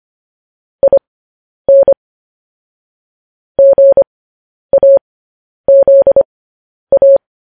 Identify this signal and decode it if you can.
Morse code, heard as a keyed tone, about 25 wpm, spaced as for 10 wpm, 550 Hz. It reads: IN GAZA